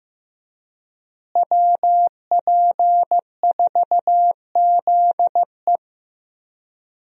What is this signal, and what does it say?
Morse code, heard as a keyed tone, about 15 wpm, 700 Hz. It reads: WP4ZE
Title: USA Call Signs - 5 Characters - Repeated 3x - 15wpm